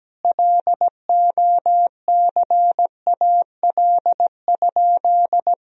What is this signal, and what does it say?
Morse code, heard as a keyed tone, about 17 wpm, 700 Hz. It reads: LOCAL?
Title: Sentences from Top 1000 Words - Review Part 3 - 17wpm